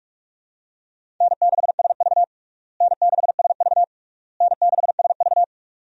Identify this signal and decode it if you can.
Morse code, heard as a keyed tone, about 46 wpm, 700 Hz. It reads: N6SV N6SV N6SV
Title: USA Call Signs - 4 Characters - Repeated 3x - 45wpm